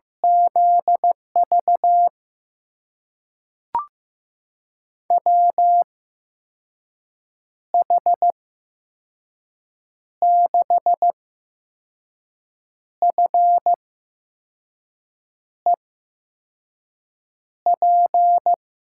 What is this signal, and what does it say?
Morse code, heard as a keyed tone, about 15 wpm, spaced as for 5 wpm, 700 Hz. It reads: ZV WH6FEP
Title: USA Call Signs - 6 Characters - 8x Character Spacing - 15wpm